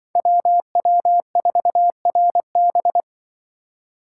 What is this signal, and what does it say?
Morse code, heard as a keyed tone, about 25 wpm, 700 Hz. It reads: WW4RB
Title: Speed Racing - USA 2x2 Call Signs 20wpm